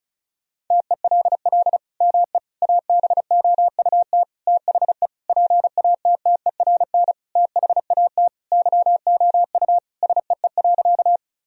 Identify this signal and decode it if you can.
Morse code, heard as a keyed tone, about 36 wpm, 700 Hz. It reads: TELL ME ABOUT THE PATTERN THAT YOU SEE.
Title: Sentences from Top 1000 Words - Review Part 2 - 35wpm